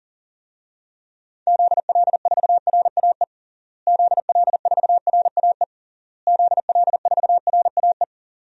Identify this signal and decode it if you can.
Morse code, heard as a keyed tone, about 40 wpm, 700 Hz. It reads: ZL4RAE ZL4RAE ZL4RAE
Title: DXCC-100 Call Signs - 6 Characters - Repeated 3x - 40wpm